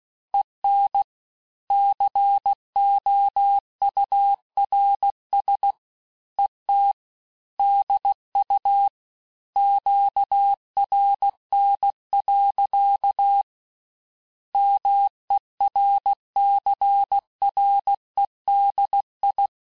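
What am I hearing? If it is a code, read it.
Morse code, heard as a keyed tone, about 16 wpm, 800 Hz. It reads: EN COURS ET DU QRN. MERCREDI
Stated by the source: Morse course text, end of QSO, sent at 15 words per minute.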